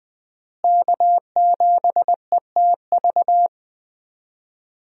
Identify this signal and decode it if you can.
Morse code, heard as a keyed tone, about 20 wpm, 700 Hz. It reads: K7ETV
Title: USA Call Signs - 5 Characters - Repeated 3x - 20wpm